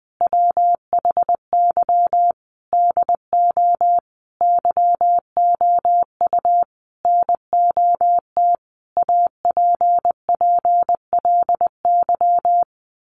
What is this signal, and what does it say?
Morse code, heard as a keyed tone, about 20 wpm, 700 Hz. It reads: WHY DO YOU NOT APPLY
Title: Sentences from Top 1200 Words 20wpm